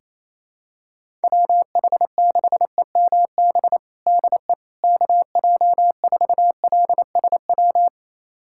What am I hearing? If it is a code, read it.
Morse code, heard as a keyed tone, about 28 wpm, 700 Hz. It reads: WH6EMB DE KJ4LSW
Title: USA Callsign DE Callsign 28wpm